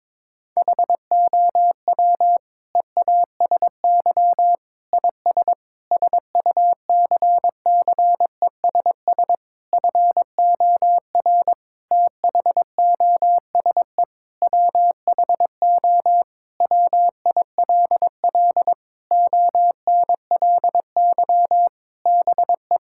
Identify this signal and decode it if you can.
Morse code, heard as a keyed tone, about 22 wpm, 700 Hz. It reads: HOW EASY IS SUCCESS FOR THOSE WHO WILL ONLY BE
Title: Sentences from Top 1000 Words - Review Part 6 - 22wpm